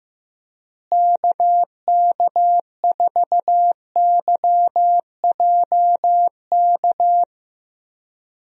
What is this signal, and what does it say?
Morse code, heard as a keyed tone, about 15 wpm, 700 Hz. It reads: KK4YJK
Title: USA Call Signs - 6 Characters - Repeated 2x - 15wpm